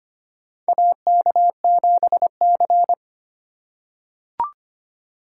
Morse code, heard as a keyed tone, about 25 wpm, 700 Hz. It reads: AK7C
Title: USA Call Signs - 4 Characters - 3x Character Spacing - 25wpm